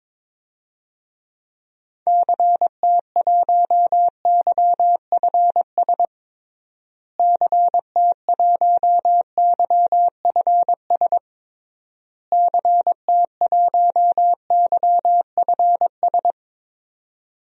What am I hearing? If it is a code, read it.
Morse code, heard as a keyed tone, about 22 wpm, 700 Hz. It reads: CT1YFS CT1YFS CT1YFS
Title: DXCC-100 Call Signs - 6 Characters - Repeated 3x - 22wpm